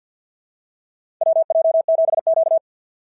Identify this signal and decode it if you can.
Morse code, heard as a keyed tone, about 52 wpm, 690 Hz. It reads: WJ7Q
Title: SOTA QSO Practice 50wpm